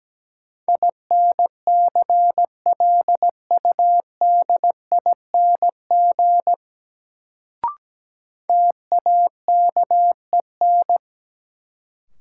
Morse code, heard as a keyed tone, about 17 wpm, 700 Hz. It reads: INCLUDING TAKEN
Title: Sentences from Top 1000 Words - Review Part 4 - New Word Drill - 17wpm